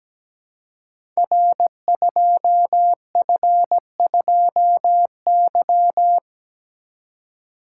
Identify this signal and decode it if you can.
Morse code, heard as a keyed tone, about 17 wpm, 700 Hz. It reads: R2F2Y
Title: Code Groups - 5 Letter:Numbers - 17wpm